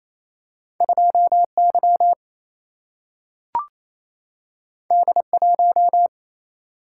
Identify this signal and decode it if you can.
Morse code, heard as a keyed tone, about 28 wpm, 700 Hz. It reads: 2Y D1
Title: Single-Letter Single-Number 28wpm